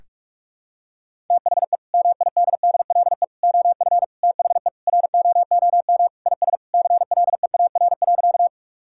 Morse code, heard as a keyed tone, about 48 wpm, 700 Hz. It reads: THE MIDDLE OF THE ROOM IS CLEAR.
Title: Sentences from Top 1000 Words - Review Part 6 - 45wpm